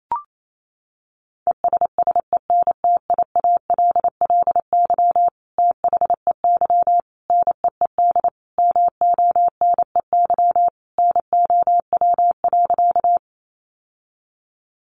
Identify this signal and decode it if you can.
Morse code, heard as a keyed tone, about 28 wpm, 700 Hz. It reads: ESSENTIALLY THEY NEED MONEY NOW.
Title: Sentences from Top 1300 Words 28wpm